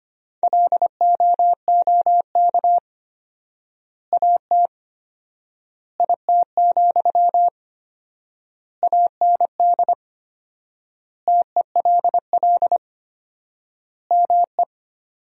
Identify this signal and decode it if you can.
Morse code, heard as a keyed tone, about 25 wpm, 700 Hz. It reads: LOOK AT IT, AND TELL ME
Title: 4x Word-Spacing - Sentences from Top 200 Words - 25wpm